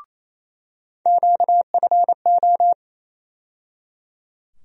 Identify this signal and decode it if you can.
Morse code, heard as a keyed tone, about 28 wpm, 700 Hz. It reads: QFO
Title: USA Call Sign Postfixes - 3 Letters 28wpm